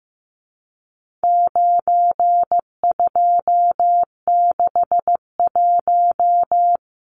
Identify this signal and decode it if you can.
Morse code, heard as a keyed tone, about 15 wpm, 700 Hz. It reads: 9261
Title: Numbers - 4-Digit 15wpm